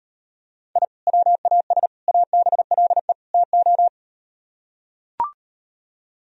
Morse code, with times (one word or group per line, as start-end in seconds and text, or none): 0.8-0.8 I
1.1-1.9 WAS
2.1-3.1 ABLE
3.3-3.9 TO